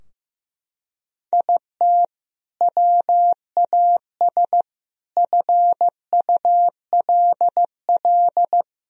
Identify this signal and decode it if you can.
Morse code, heard as a keyed tone, about 15 wpm, 700 Hz. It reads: IT WAS FULL
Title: Sentences from Top 1000 Words - Review Part 3 - 15wpm